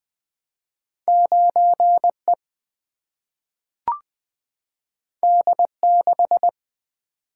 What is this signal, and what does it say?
Morse code, heard as a keyed tone, about 20 wpm, 700 Hz. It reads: 9E D6
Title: Single-Letter Single-Number 20wpm